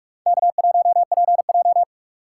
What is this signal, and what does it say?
Morse code, heard as a keyed tone, about 45 wpm, 700 Hz. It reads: K1PJ